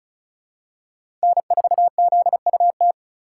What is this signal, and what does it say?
Morse code, heard as a keyed tone, about 36 wpm, 700 Hz. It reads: N4ZUT
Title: USA Call Signs - 5 Characters - 2x Character Spacing - 35wpm